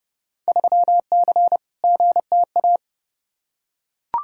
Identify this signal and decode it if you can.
Morse code, heard as a keyed tone, about 30 wpm, 700 Hz. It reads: 3C GTA
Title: Field Day Exchanges Encore - 30wpm